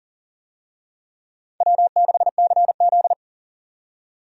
Morse code, heard as a keyed tone, about 40 wpm, 700 Hz. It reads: W6CZ
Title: USA Call Signs - 4 Characters - 40wpm